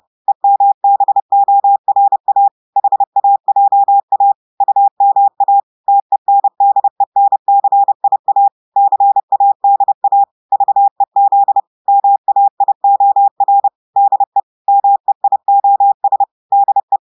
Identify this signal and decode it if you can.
Morse code, heard as a keyed tone, about 30 wpm, 800 Hz. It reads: EMBORA HAJA UMA TENDENCIA CADA VEZ MAIOR DE MEIOS DE
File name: portugais texte en clair 2 vitesse 30 mots minutes